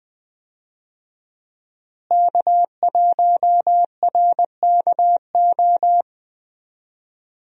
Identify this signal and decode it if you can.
Morse code, heard as a keyed tone, about 20 wpm, 700 Hz. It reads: K1RKO
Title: USA Call Signs - 5 Characters - Repeated 2x - 20wpm